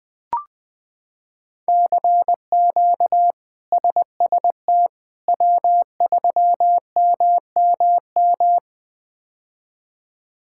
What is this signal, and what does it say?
Morse code, heard as a keyed tone, about 20 wpm, 700 Hz. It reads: CQ SST W3MMM
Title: K1USN SST QSO Practice - 20wpm